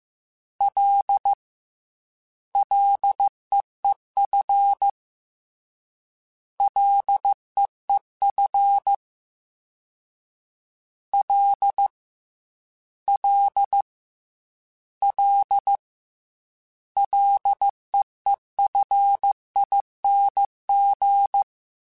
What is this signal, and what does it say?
Morse code, heard as a keyed tone, about 15 wpm, 800 Hz. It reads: L LEEF LEEF L L L LEEFING
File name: F and L / F and L mp3 / F and L Prefixes 3x2 Twice @13